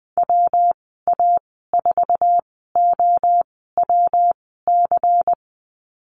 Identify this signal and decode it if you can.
Morse code, heard as a keyed tone, about 20 wpm, spaced as for 14 wpm, 700 Hz. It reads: WA4OWC
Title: 173 Lesson 28g - Number 9 - Call Signs 20wpm